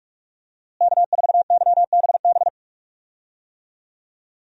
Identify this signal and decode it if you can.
Morse code, heard as a keyed tone, about 42 wpm, 700 Hz. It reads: K4YBB
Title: USA Call Signs - 5 Characters - Repeated 2x - 45wpm